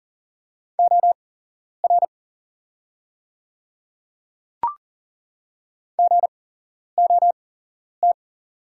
Morse code, heard as a keyed tone, about 38 wpm, spaced as for 11 wpm, 700 Hz. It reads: OR GOT
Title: 8x Character Spacing - Top 300 Words - 40wpm